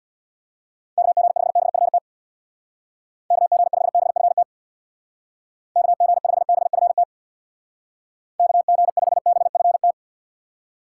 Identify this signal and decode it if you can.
Morse code, heard as a keyed tone, about 59 wpm, 700 Hz. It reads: KG5BFT KG5BFT KG5BFT KG5BFT
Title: Speed Racing - USA 2x3 Call Signs 50wpm